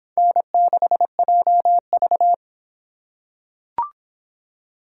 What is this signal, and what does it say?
Morse code, heard as a keyed tone, about 22 wpm, 700 Hz. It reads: N6JV